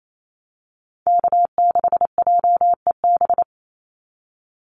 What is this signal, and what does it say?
Morse code, heard as a keyed tone, about 28 wpm, 700 Hz. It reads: K6JEB
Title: Call Signs - 28wpm